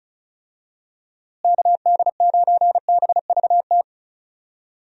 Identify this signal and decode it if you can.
Morse code, heard as a keyed tone, about 34 wpm, 700 Hz. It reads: KD9BVT
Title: USA Call Signs - 6 Characters - Repeated 2x - 35wpm